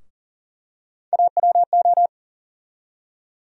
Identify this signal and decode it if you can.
Morse code, heard as a keyed tone, about 40 wpm, 700 Hz. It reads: AWO
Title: USA Call Sign Postfixes - 3 Letters 40wpm